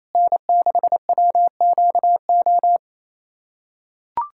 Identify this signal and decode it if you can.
Morse code, heard as a keyed tone, about 28 wpm, 700 Hz. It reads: N6WQO
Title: USA Callsign DE Callsign 28wpm